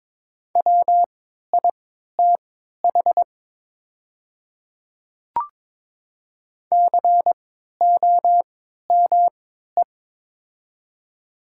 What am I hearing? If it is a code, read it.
Morse code, heard as a keyed tone, about 22 wpm, spaced as for 13 wpm, 700 Hz. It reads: WITH COME
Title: 3x Character Spacing - Top 100 Words - 22wpm